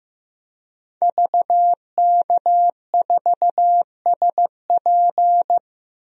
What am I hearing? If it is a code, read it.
Morse code, heard as a keyed tone, about 15 wpm, 700 Hz. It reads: VK4SP
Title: DXCC-100 Call Signs - 5 Characters - 15wpm